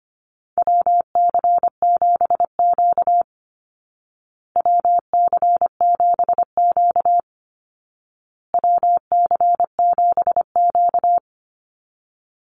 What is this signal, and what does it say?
Morse code, heard as a keyed tone, about 25 wpm, 700 Hz. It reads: WC7Q WC7Q WC7Q
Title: Call Signs Repeated 2x - 25wpm